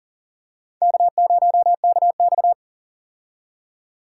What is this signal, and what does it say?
Morse code, heard as a keyed tone, about 40 wpm, 700 Hz. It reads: K0KX
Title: K1USN SST QSO Practice - 40wpm